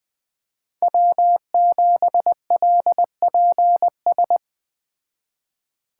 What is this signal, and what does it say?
Morse code, heard as a keyed tone, about 19 wpm, 700 Hz. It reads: W7LPS
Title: Speed Racing - USA 1x3 Call Signs 17wpm